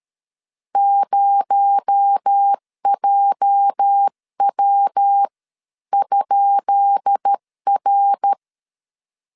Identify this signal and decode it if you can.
Morse code, heard as a keyed tone, about 13 wpm, spaced as for 11 wpm, 800 Hz. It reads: 0JW?R
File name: LEZIONI / 10 SIGNS / 10B INT 12wpm 2414